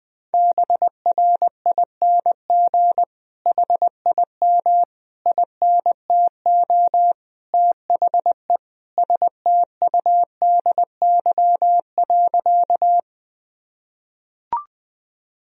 Morse code, heard as a keyed tone, about 20 wpm, 700 Hz. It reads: BRING HIM INTO THE STUDY.
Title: Sentences from Top 1000 Words - Review Part 1 - 20wpm